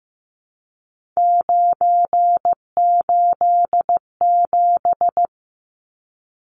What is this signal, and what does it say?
Morse code, heard as a keyed tone, about 15 wpm, 700 Hz. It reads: 987
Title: Numbers - 3-Digit 15wpm